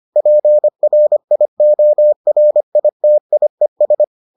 Morse code, heard as a keyed tone, about 25 wpm, 600 Hz. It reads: PRIORITIES